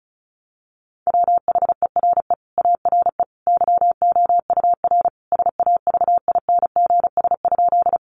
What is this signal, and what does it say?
Morse code, heard as a keyed tone, about 34 wpm, 700 Hz. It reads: WHERE ARE YOUR SAVINGS?
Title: Sentences from Top 1200 Words 35wpm